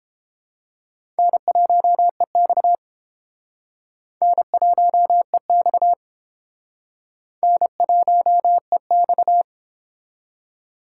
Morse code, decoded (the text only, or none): N1EX N1EX N1EX